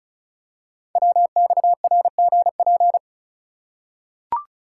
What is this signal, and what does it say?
Morse code, heard as a keyed tone, about 36 wpm, 700 Hz. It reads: WXRGP
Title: Code Groups - 5 Letters - 35wpm